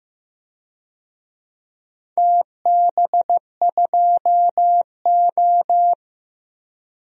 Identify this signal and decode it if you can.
Morse code, heard as a keyed tone, about 15 wpm, 700 Hz. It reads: TB2O
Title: DXCC-100 Call Signs - 4 Characters - 15wpm